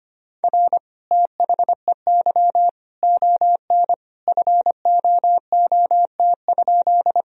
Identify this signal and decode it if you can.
Morse code, heard as a keyed tone, about 25 wpm, 700 Hz. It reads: R THEY ON FOOT?
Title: Sentences from Top 500 Words and Abbreviations 25wpm